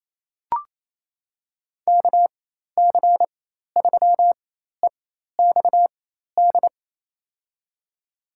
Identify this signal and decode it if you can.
Morse code, heard as a keyed tone, about 28 wpm, spaced as for 13 wpm, 700 Hz. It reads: KC3EXD
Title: USA Call Signs - 6 Characters - 4x Character Spacing - 28wpm